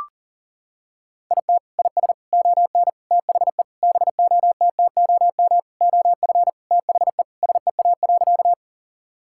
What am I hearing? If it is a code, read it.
Morse code, heard as a keyed tone, about 40 wpm, 700 Hz. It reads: IT IS ON THE BOTTOM OF THE SEA.
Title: Sentences from Top 1000 Words - Review Part 6 - 40wpm